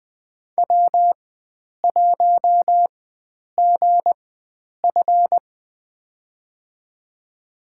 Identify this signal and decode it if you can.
Morse code, heard as a keyed tone, about 20 wpm, spaced as for 9 wpm, 700 Hz. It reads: W1GF
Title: USA Call Signs - 4x Character Spacing - 20wpm